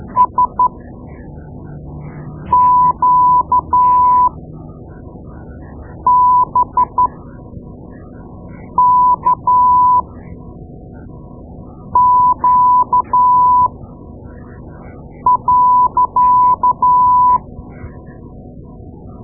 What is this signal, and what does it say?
Morse code, heard as a keyed tone, about 12 wpm, spaced as for 4 wpm, 1000 Hz. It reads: SQBKQ.
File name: Morse Lessons / Morse Lessons / Tape-2A